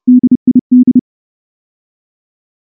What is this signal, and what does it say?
Morse code, heard as a keyed tone, about 29 wpm, 250 Hz. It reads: DID
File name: morse / morse-training-30WPM